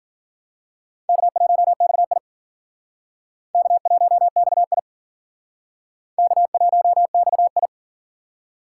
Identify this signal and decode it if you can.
Morse code, heard as a keyed tone, about 47 wpm, 700 Hz. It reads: K1XI K1XI K1XI
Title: Speed Racing - USA 1x2 Call Signs 40wpm